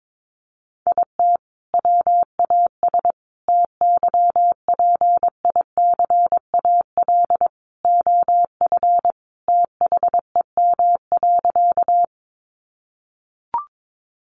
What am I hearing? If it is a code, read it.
Morse code, heard as a keyed tone, about 22 wpm, 700 Hz. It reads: IT WAS TYPICAL OF THEM.
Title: Sentences from Top 1100 Words 22wpm